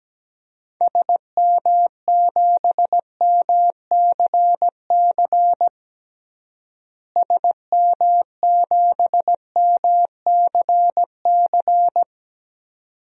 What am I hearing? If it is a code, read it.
Morse code, heard as a keyed tone, about 17 wpm, 700 Hz. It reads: SM7MCC SM7MCC
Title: DXCC-100 Call Signs - 6 Characters - Repeated 2x - 17wpm